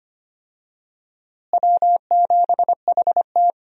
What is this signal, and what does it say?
Morse code, heard as a keyed tone, about 25 wpm, 700 Hz. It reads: W7HT
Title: USA Call Signs - 4 Characters - Repeated 3x - 25wpm